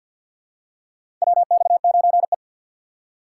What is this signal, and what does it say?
Morse code, heard as a keyed tone, about 51 wpm, 700 Hz. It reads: WX9E
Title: USA Call Signs - 4 Characters - 6x Character Spacing - 50wpm